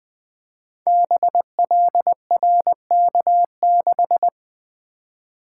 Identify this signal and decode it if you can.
Morse code, heard as a keyed tone, about 20 wpm, 700 Hz. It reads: BLRK6
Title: Code Groups - 5 Letter:Numbers - 20wpm